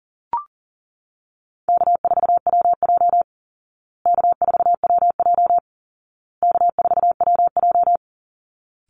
Morse code, heard as a keyed tone, about 40 wpm, 700 Hz. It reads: K4WJ K4WJ K4WJ
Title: Call Signs Repeated 2x - 40wpm